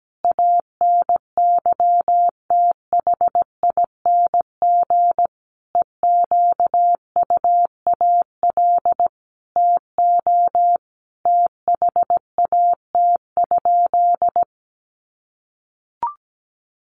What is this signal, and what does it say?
Morse code, heard as a keyed tone, about 17 wpm, 700 Hz. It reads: ANYTHING EQUAL TO THAT?
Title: Sentences from Top 1400 Words 17wpm